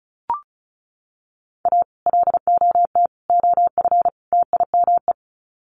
Morse code, heard as a keyed tone, about 34 wpm, 700 Hz. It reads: A LOT OF TIME
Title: Sets of 4 Words 35wpm